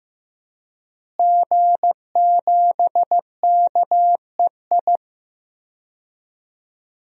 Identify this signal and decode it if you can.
Morse code, heard as a keyed tone, about 15 wpm, 700 Hz. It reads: G7KEI
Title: DXCC-100 Call Signs - 5 Characters - Repeated 3x - 15wpm